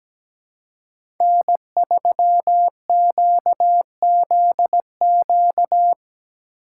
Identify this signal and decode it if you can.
Morse code, heard as a keyed tone, about 17 wpm, 700 Hz. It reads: N3QZQ
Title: USA Call Signs - 5 Characters - 6x Character Spacing - 17wpm